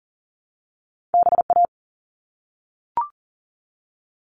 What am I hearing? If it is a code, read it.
Morse code, heard as a keyed tone, about 41 wpm, 700 Hz. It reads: BA
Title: All Cut Numbers - 2-Digit 40wpm